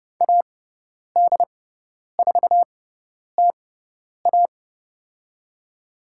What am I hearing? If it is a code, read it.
Morse code, heard as a keyed tone, about 30 wpm, spaced as for 10 wpm, 700 Hz. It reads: AD4TA
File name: Lesson 18g - Letter m - Call Signs 30wpm